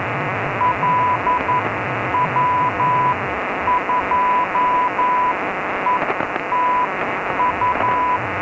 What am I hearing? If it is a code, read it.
Morse code, heard as a keyed tone, about 11 wpm, spaced as for 9 wpm, 990 Hz, with radio noise. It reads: LW2ETU
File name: Notas varias / LW2ETU 137 kHz / lw2etu 30 12 03